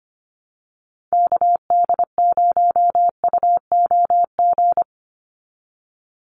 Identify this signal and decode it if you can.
Morse code, heard as a keyed tone, about 25 wpm, 700 Hz. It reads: KD0UOG
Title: USA Call Signs 25wpm